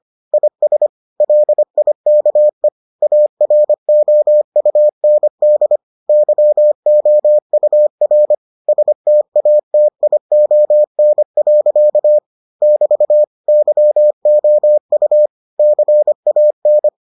Morse code, heard as a keyed tone, about 25 wpm, 600 Hz. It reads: IS LIKE AROUND YOUR STATION. <BT> YOU CAN